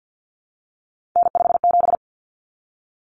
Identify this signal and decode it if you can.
Morse code, heard as a keyed tone, about 50 wpm, 700 Hz. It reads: N57
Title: Common Cut Numbers - 3-Digit 50wpm